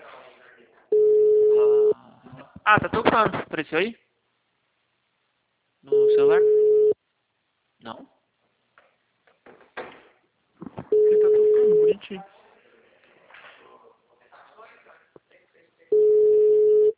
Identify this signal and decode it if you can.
Ringback tone (Europe, 425 Hz), with speech.